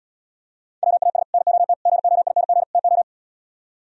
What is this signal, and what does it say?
Morse code, heard as a keyed tone, about 77 wpm, 700 Hz. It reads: WAS NOT GOING TO